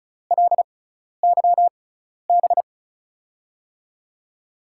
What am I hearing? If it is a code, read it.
Morse code, heard as a keyed tone, about 36 wpm, spaced as for 12 wpm, 700 Hz. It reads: LYB
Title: USA Call Signs - 6x Character Spacing - 35wpm